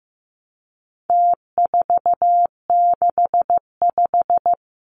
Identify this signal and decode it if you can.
Morse code, heard as a keyed tone, about 15 wpm, 700 Hz. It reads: T465
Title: Numbers - Contest Style - 4-Digit 15wpm